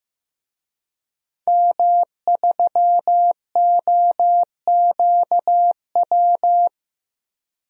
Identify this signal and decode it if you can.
Morse code, heard as a keyed tone, about 15 wpm, 700 Hz. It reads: M3OQW
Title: DXCC-100 Call Signs - 5 Characters - 15wpm